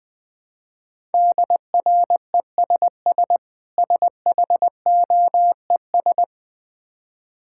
Morse code, heard as a keyed tone, about 20 wpm, 700 Hz. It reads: DRESS SHOES
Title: Difficult Words with Hints - 20wpm